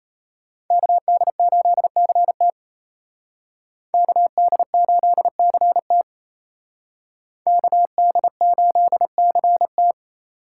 Morse code, heard as a keyed tone, about 34 wpm, 700 Hz. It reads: KD8CT KD8CT KD8CT
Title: Speed Racing - USA 2x2 Call Signs 28wpm